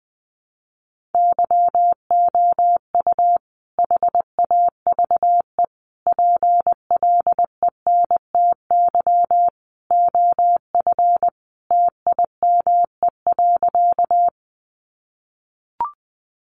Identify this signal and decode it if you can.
Morse code, heard as a keyed tone, about 20 wpm, 700 Hz. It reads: YOU HAVE PLENTY OF TIME.
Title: Sentences from Top 1300 Words 20wpm